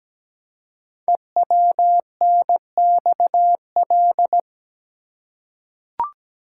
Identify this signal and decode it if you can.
Morse code, heard as a keyed tone, about 17 wpm, 700 Hz. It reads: EWNXL
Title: Code Groups - 5 Letters - 17wpm